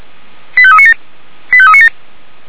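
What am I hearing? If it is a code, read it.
Five-tone selective call (ZVEI), heard as quick runs of five tones, tones of about 70 ms. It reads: 85397 85397
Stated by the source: ZVEI five-tone selective call received by SDR.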